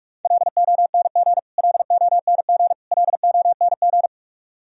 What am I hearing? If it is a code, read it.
Morse code, heard as a keyed tone, about 45 wpm, 700 Hz. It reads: LONG LONG LONG
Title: Top 100 Words Repeated 45wpm